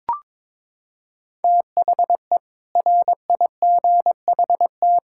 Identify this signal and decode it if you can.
Morse code, heard as a keyed tone, about 22 wpm, 700 Hz. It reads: THE RIGHT
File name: practice / 22wpm / Sets of 6 Words - 22wpm